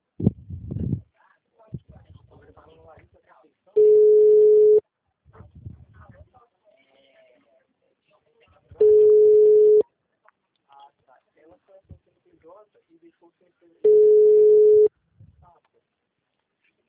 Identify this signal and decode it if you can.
Ringback tone (Europe, 425 Hz), with speech.
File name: out- -2012- - - .4836